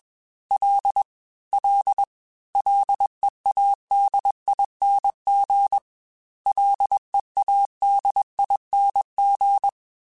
Morse code, heard as a keyed tone, about 21 wpm, 800 Hz. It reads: L L LEADING LEADING
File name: Ls / Single L m3s / Single L Prefixe 3x2 Twice @20